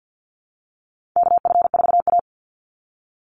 Morse code, heard as a keyed tone, about 50 wpm, 700 Hz. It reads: KF4A
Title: Call Signs Repeated 2x - 50wpm